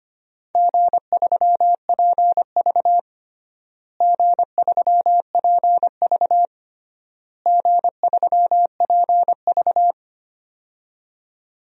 Morse code, heard as a keyed tone, about 25 wpm, 700 Hz. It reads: G3PV G3PV G3PV
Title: DXCC-100 Call Signs - 4 Characters - Repeated 3x - 25wpm